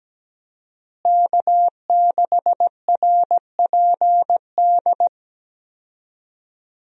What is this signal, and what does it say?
Morse code, heard as a keyed tone, about 17 wpm, 700 Hz. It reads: K6RPD
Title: USA Call Signs - 5 Characters - Repeated 2x - 17wpm